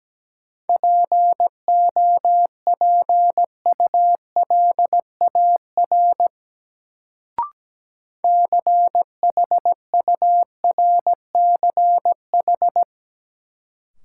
Morse code, heard as a keyed tone, about 17 wpm, 700 Hz. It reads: POPULAR CHURCH
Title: Sets of 5 Words - New Word Drill - 17wpm